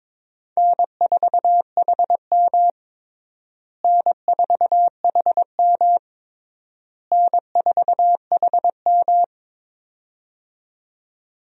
Morse code, heard as a keyed tone, about 22 wpm, 700 Hz. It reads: N4HM N4HM N4HM
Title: USA Call Signs - 4 Characters - Repeated 3x - 22wpm